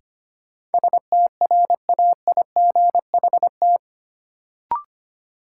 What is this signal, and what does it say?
Morse code, heard as a keyed tone, about 25 wpm, 700 Hz. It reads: STRAIGHT
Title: Sentences from Top 1000 Words - New Word Drill - 25wpm